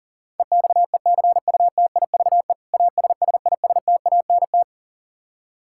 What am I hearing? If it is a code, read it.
Morse code, heard as a keyed tone, about 40 wpm, 700 Hz. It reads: EXECUTIVE ASSISTANT
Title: Practice QSO - Exchange 3 - More Info - 40wpm